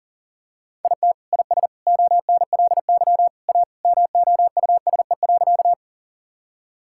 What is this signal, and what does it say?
Morse code, heard as a keyed tone, about 40 wpm, 700 Hz. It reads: IT IS ONLY A MOUSE.
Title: Sentences from Top 1000 Words - Review Part 5 - 40wpm